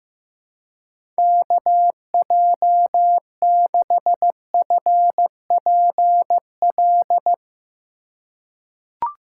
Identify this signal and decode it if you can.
Morse code, heard as a keyed tone, about 15 wpm, 700 Hz. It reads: KJ6FPL
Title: USA Call Signs - 6 Characters - 2x Character Spacing - 15wpm